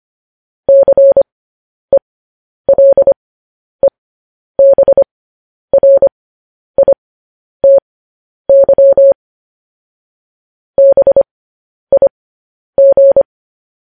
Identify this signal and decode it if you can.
Morse code, heard as a keyed tone, about 25 wpm, spaced as for 10 wpm, 550 Hz. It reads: CELEBRITY BIG
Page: news headlines in morse code at 10 wpm